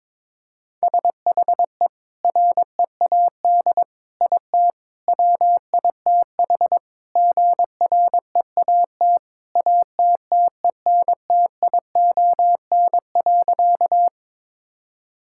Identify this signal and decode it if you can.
Morse code, heard as a keyed tone, about 22 wpm, 700 Hz. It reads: SHE READ IT WITH GREAT ATTENTION.
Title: Sentences from Top 1000 Words - Review Part 6 - 22wpm